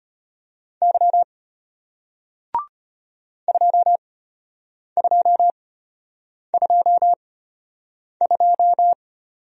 Morse code, heard as a keyed tone, about 33 wpm, spaced as for 8 wpm, 700 Hz. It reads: Y 2222